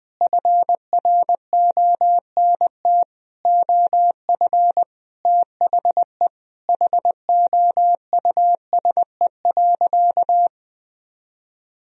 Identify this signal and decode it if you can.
Morse code, heard as a keyed tone, about 20 wpm, 700 Hz. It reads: FRONT OF THE HOUSE.
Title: Sentences from Top 1000 Words - Review Part 2 - 20wpm